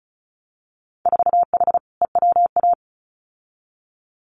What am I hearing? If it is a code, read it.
Morse code, heard as a keyed tone, about 34 wpm, 700 Hz. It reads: 4H EWA
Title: Winter Field Day Exchanges Encore 35wpm